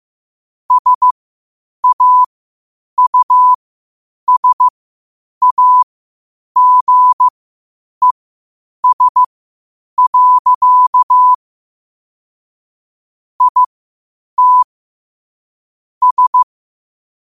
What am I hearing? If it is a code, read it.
Morse code, heard as a keyed tone, about 15 wpm, spaced as for 8 wpm, 1000 Hz. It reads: SAUSAGES. IT S
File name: qotdfiles / 8wpm